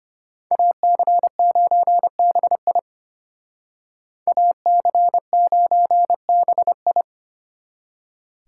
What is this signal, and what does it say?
Morse code, heard as a keyed tone, about 32 wpm, 700 Hz. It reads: AC9BI AC9BI